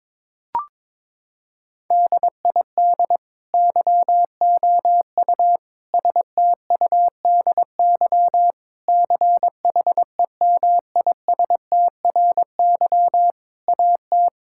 Morse code, heard as a keyed tone, about 22 wpm, 700 Hz. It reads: DID YOU STUDY CHEMISTRY AT
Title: Sentences from Top 1000 Words - Review Part 6 - 22wpm